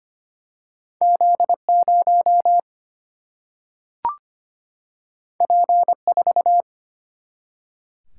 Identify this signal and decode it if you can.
Morse code, heard as a keyed tone, about 25 wpm, 700 Hz. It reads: Z0 P4
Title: Single-Letter Single-Number 25wpm